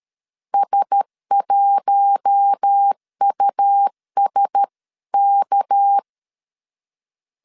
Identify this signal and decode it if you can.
Morse code, heard as a keyed tone, about 13 wpm, 800 Hz. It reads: S1USK
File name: CW / DET / 1 9 1500C S B 12wpm 3819